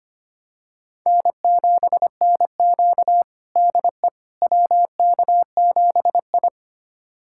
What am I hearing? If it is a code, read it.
Morse code, heard as a keyed tone, about 25 wpm, 700 Hz. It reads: N7NQ DE WK7I
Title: USA Callsign DE Callsign - 4 Characters - 25wpm